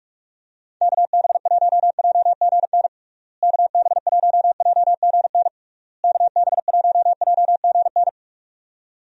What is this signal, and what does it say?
Morse code, heard as a keyed tone, about 46 wpm, 700 Hz. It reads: KB1JGN KB1JGN KB1JGN
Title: USA Call Signs - 6 Characters - Repeated 3x - 45wpm